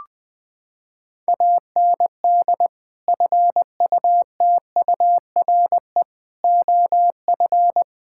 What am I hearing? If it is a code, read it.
Morse code, heard as a keyed tone, about 20 wpm, 700 Hz. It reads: AND FUTURE OF
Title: Sentences from Top 1600 Words 20wpm